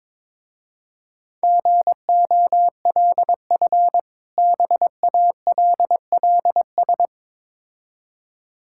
Morse code, heard as a keyed tone, about 23 wpm, 690 Hz. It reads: GOLF BALLS
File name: Difficult Words with Hints - 22wpm